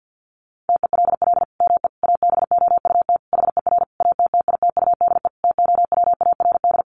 Morse code, heard as a keyed tone, about 50 wpm, 700 Hz. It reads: TELL ME ABOUT HER ATTITUDE TOWARD
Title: Sentences from Top 1300 Words 50wpm